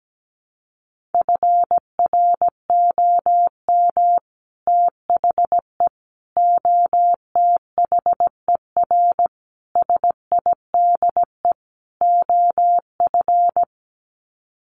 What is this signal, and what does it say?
Morse code, heard as a keyed tone, about 17 wpm, 700 Hz. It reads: FROM THE OTHER SIDE OF
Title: Sets of 5 Words Encore 17wpm